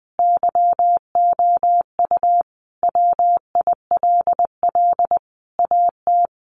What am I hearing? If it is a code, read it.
Morse code, heard as a keyed tone, about 20 wpm, 700 Hz. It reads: YOU WILL AT
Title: Sentences from Top 1500 Words 20wpm